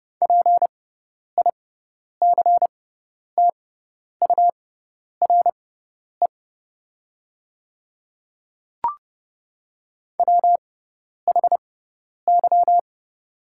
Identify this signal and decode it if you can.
Morse code, heard as a keyed tone, about 30 wpm, spaced as for 10 wpm, 700 Hz. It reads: PICTURE WHY
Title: 6x Character Spacing - Top 300 Words - 30wpm